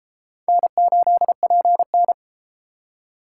Morse code, heard as a keyed tone, about 30 wpm, 700 Hz. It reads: N8PN